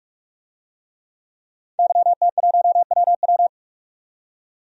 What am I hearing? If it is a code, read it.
Morse code, heard as a keyed tone, about 43 wpm, 700 Hz. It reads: YT1WW
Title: DXCC-100 Call Signs - 5 Characters - 45wpm